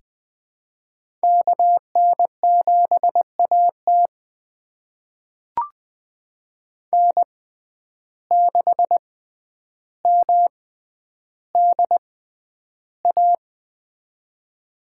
Morse code, heard as a keyed tone, about 20 wpm, spaced as for 10 wpm, 700 Hz. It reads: KN7AT N6MDA
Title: USA Call Signs - 5 Characters - 6x Character Spacing - 20wpm